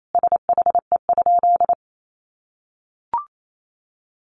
Morse code, heard as a keyed tone, about 28 wpm, 700 Hz. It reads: SHE?
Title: Sentences from Top 200 Words 28wpm